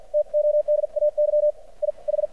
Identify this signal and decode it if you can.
Morse code, heard as a keyed tone, about 53 wpm, spaced as for 45 wpm, 600 Hz, with radio noise. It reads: TODAY I H
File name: archived / CRSnet